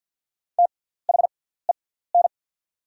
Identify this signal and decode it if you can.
Morse code, heard as a keyed tone, about 48 wpm, spaced as for 17 wpm, 700 Hz. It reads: THEN
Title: 6x Character Spacing - Top 200 Words - 50wpm